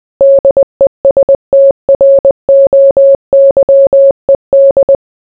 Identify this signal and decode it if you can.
Morse code, heard as a keyed tone, about 20 wpm, 550 Hz. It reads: DESTROYED